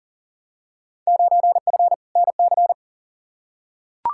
Morse code, heard as a keyed tone, about 40 wpm, 700 Hz. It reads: 9F NC
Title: Field Day Exchanges Encore - 40wpm